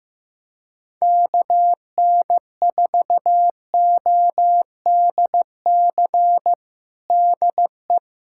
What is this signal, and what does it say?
Morse code, heard as a keyed tone, about 15 wpm, 700 Hz. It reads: KN4ODC DE
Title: USA Callsign DE Callsign 15wpm